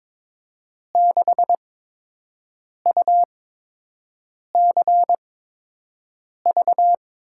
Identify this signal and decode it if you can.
Morse code, heard as a keyed tone, about 22 wpm, spaced as for 6 wpm, 700 Hz. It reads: 6UCV
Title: USA Call Signs - 8x Character Spacing - 22wpm